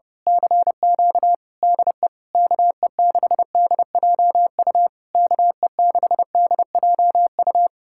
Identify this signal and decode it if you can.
Morse code, heard as a keyed tone, about 27 wpm, 700 Hz. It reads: CQ DE KE6DJU KE6DJU
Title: Speed-Racing Common QSO Phrases - 30wpm